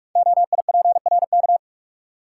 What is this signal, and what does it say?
Morse code, heard as a keyed tone, about 44 wpm, 690 Hz. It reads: OIPRK
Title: Code Groups - 5 Letters - 45wpm